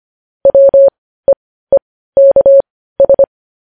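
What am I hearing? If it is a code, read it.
Morse code, heard as a keyed tone, about 25 wpm, spaced as for 15 wpm, 550 Hz. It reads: WEEKS
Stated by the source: Morse news headlines practice at 15.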